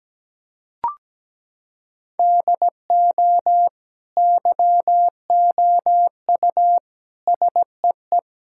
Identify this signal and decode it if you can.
Morse code, heard as a keyed tone, about 17 wpm, 700 Hz. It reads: DO YOU SEE
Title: Sentences from Top 1600 Words 17wpm